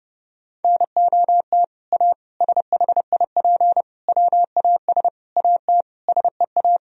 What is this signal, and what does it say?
Morse code, heard as a keyed tone, about 32 wpm, 700 Hz. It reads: NOT A SHIP WAS AT SEA
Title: Sentences from Top 1000 Words - Review Part 2 - 30wpm